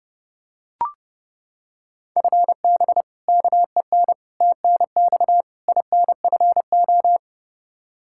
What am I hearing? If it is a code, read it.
Morse code, heard as a keyed tone, about 30 wpm, 700 Hz. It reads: FB KEN TNX INFO
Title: Practice QSO - Exchange 3 - More Info - 30wpm